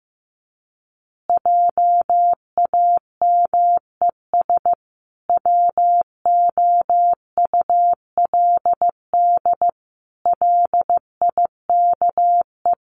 Morse code, heard as a keyed tone, about 14 wpm, 700 Hz. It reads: JAMES WOULD LIKE
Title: Easy Phrases 15wpm